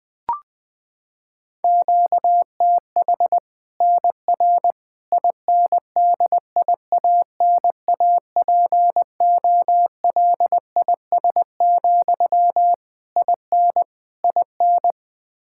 Morse code, heard as a keyed tone, about 20 wpm, 700 Hz. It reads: QTH NR INDIANAPOLIS, IN IN
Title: Practice QSO - Exchange 1 - Beginning - 20wpm